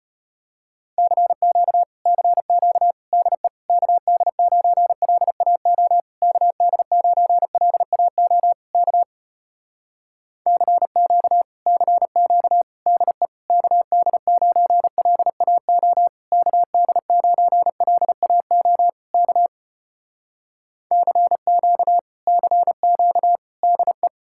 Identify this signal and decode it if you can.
Morse code, heard as a keyed tone, about 33 wpm, 700 Hz. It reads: CQ CQ DE KD9LAO KD9LAO K CQ CQ DE KD9LAO KD9LAO K CQ CQ DE